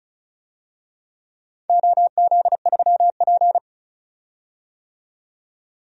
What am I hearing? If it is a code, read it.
Morse code, heard as a keyed tone, about 36 wpm, 690 Hz. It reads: OZ3P
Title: DXCC-100 Call Signs - 4 Characters - 35wpm